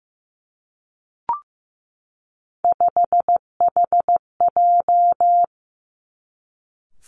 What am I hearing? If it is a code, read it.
Morse code, heard as a keyed tone, about 15 wpm, 700 Hz. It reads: 5HJ
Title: Three Challenging Letter-Number Combinations 15wpm